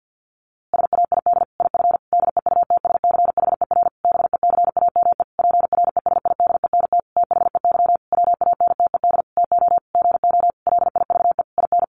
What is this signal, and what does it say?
Morse code, heard as a keyed tone, about 50 wpm, 700 Hz. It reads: SAID IF DEUTSCHER BECAME PRESIDENT THEY WANTED TO GO LIVE IN
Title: A Sound Of Thunder 50wpm